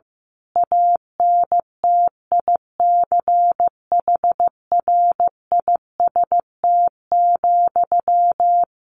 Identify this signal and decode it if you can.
Morse code, heard as a keyed tone, about 15 wpm, 700 Hz. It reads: ANTICHRIST,
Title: A Sound Of Thunder 15wpm